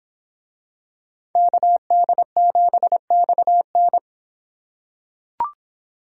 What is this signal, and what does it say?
Morse code, heard as a keyed tone, about 25 wpm, 700 Hz. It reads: KD7XN